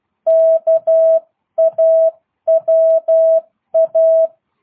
Morse code, heard as a keyed tone, about 12 wpm, 660 Hz. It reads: KAWA
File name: jh8cbh / 2022cw